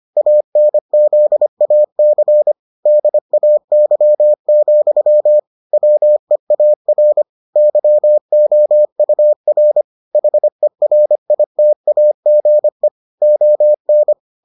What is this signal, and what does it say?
Morse code, heard as a keyed tone, about 25 wpm, 600 Hz. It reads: ANZAC DAY, WEAR YOUR HERITAGE ON